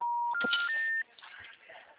Special information tone.
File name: out- -2004